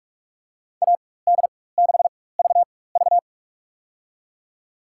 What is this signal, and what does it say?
Morse code, heard as a keyed tone, about 46 wpm, spaced as for 21 wpm, 700 Hz. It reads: AD6VV
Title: USA Call Signs - 5 Characters - 4x Character Spacing - 45wpm